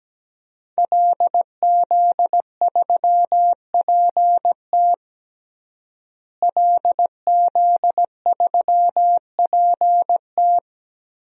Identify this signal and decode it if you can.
Morse code, heard as a keyed tone, about 17 wpm, 700 Hz. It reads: LZ3PT LZ3PT
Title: DXCC-100 Call Signs - 5 Characters - Repeated 3x - 17wpm